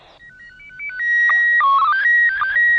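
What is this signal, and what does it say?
SSTV (slow-scan television), PD 120.
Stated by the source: SSTV transmission from the International Space Station.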